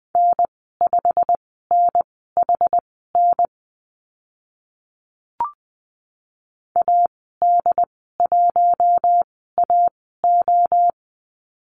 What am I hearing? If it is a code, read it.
Morse code, heard as a keyed tone, about 20 wpm, spaced as for 14 wpm, 700 Hz. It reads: N5NHN AD1AO
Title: 088 Lesson 15g - Number 5 - Call Signs 20wpm